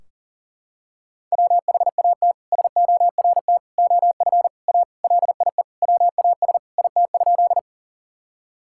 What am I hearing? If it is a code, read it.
Morse code, heard as a keyed tone, about 40 wpm, 700 Hz. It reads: WHAT SORT OF A LIE WAS IT?
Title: Sentences from Top 1600 Words 40wpm